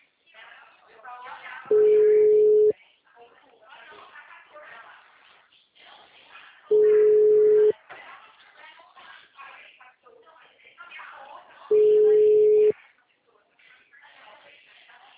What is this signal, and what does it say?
Ringback tone (Europe, 425 Hz), with speech.